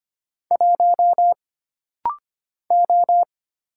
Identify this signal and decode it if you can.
Morse code, heard as a keyed tone, about 25 wpm, 700 Hz. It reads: 1 O
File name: Single Character (Letter:Number:Punctuation:Prosign) - Rapid-Fire - 25wpm